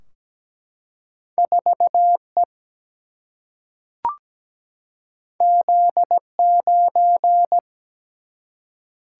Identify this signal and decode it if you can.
Morse code, heard as a keyed tone, about 17 wpm, 700 Hz. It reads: 4 E Z 9
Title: Single-Letter Single-Number 17wpm